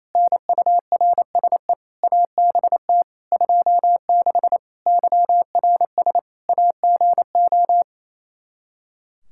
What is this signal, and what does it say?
Morse code, heard as a keyed tone, about 27 wpm, 700 Hz. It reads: NURSE ABT 26 YRS AGO
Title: Practice QSO - Exchange 3 - More Info - 28wpm